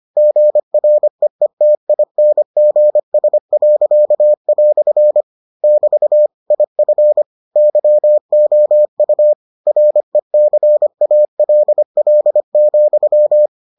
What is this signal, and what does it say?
Morse code, heard as a keyed tone, about 25 wpm, 600 Hz. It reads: GREETINGS." <BT> IF YOU RECALL,